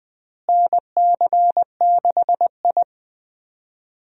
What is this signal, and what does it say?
Morse code, heard as a keyed tone, about 20 wpm, 700 Hz. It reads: NC6I
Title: USA Call Signs - 4 Characters - Repeated 2x - 20wpm